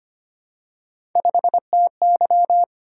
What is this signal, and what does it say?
Morse code, heard as a keyed tone, about 25 wpm, 700 Hz. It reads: 5TY
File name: Three Letter-Number Combinations - 25wpm